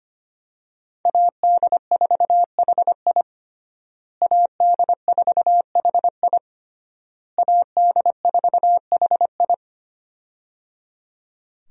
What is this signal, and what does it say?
Morse code, heard as a keyed tone, about 25 wpm, 700 Hz. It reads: AD4HI AD4HI AD4HI
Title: USA Call Signs - 5 Characters - Repeated 3x - 25wpm